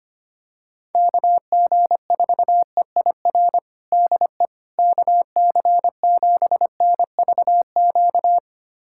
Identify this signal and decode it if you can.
Morse code, heard as a keyed tone, about 25 wpm, 700 Hz. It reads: KG4EIR DE KC7NVQ
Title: USA Callsign DE Callsign 25wpm